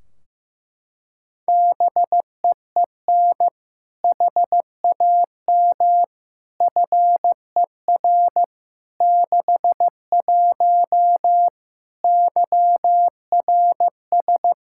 Morse code, heard as a keyed tone, about 15 wpm, 700 Hz. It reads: BEEN HAM FER 61 YRS
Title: K1USN SST QSO Practice 15wpm